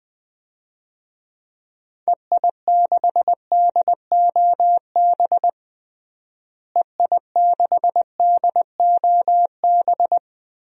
Morse code, heard as a keyed tone, about 20 wpm, 700 Hz. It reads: EI6DOB EI6DOB
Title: DXCC-100 Call Signs - 6 Characters - Repeated 3x - 20wpm